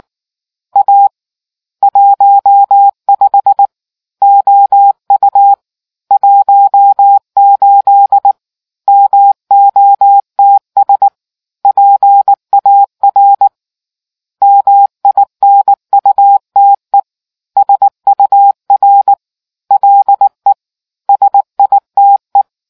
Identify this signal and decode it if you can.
Morse code, heard as a keyed tone, about 19 wpm, 800 Hz. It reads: A 15 OU 18 MOTS PAR MINUTE SUR LE SITE